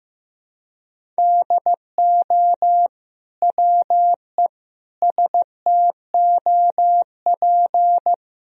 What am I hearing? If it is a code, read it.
Morse code, heard as a keyed tone, about 15 wpm, 700 Hz. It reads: DO WE STOP
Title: Sentences from Top 300 Words and Abbreviations 15wpm